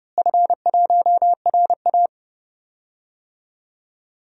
Morse code, heard as a keyed tone, about 30 wpm, 700 Hz. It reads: F1RA
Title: DXCC-100 Call Signs - 4 Characters - Repeated 3x - 30wpm